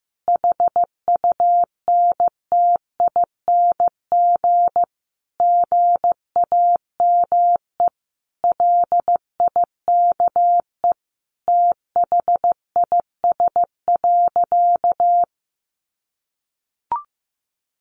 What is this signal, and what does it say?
Morse code, heard as a keyed tone, about 15 wpm, 700 Hz. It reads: HUNTING GAME LIKE THIS.
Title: A Sound Of Thunder 15wpm